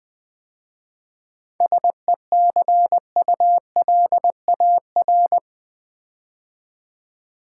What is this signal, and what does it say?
Morse code, heard as a keyed tone, about 20 wpm, 700 Hz. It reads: SECULAR
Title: 7 Letter Words - First 3 Letters Given - 20wpm